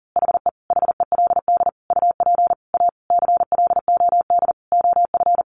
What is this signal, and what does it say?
Morse code, heard as a keyed tone, about 40 wpm, 700 Hz. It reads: HE HELD UP A CLOD OF
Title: A Sound Of Thunder 40wpm